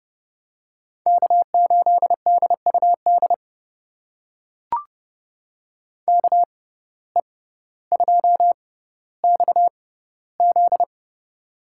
Morse code, heard as a keyed tone, about 30 wpm, spaced as for 15 wpm, 700 Hz. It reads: K8DUD KE2XZ
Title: USA Call Signs - 5 Characters - 6x Character Spacing - 30wpm